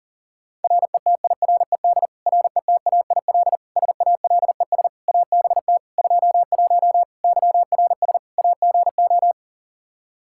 Morse code, heard as a keyed tone, about 40 wpm, 700 Hz. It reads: RETIRED RETAIL SALES ABT 21 YRS AGO
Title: Practice QSO - Exchange 3 - More Info - 40wpm